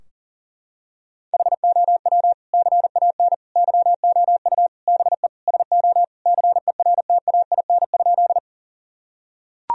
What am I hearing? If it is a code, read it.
Morse code, heard as a keyed tone, about 40 wpm, 700 Hz. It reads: HOW CAN YOU BE SO CERTAIN?
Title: Sentences from Top 1000 Words - Review Part 2 - 40wpm